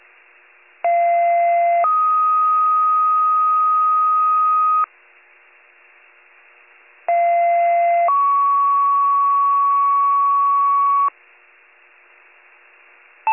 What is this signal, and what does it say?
Two-tone page, heard as a one-second tone, then a three-second tone, about 690 then 1250 Hz, about 690 then 1120 Hz. It is heard from 0.8 to 11.1 s.